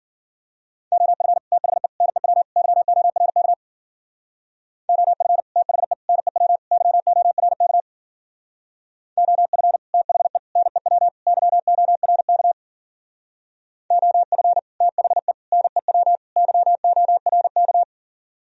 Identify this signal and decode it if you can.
Morse code, heard as a keyed tone, about 49 wpm, 700 Hz. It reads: OF THE NEW YORK OF THE NEW YORK OF THE NEW YORK OF THE NEW YORK
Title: Speed-Racing - Sets of 4 Words 40wpm